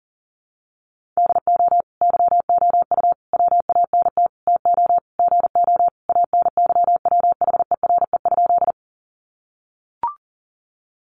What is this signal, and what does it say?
Morse code, heard as a keyed tone, about 40 wpm, 700 Hz. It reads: DO YOU WANT TO GO ANYWHERE?
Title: Sentences from Top 1400 Words 40wpm